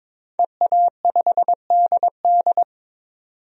Morse code, heard as a keyed tone, about 22 wpm, 700 Hz. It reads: EA5DD
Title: DXCC-100 Call Signs - 5 Characters - 22wpm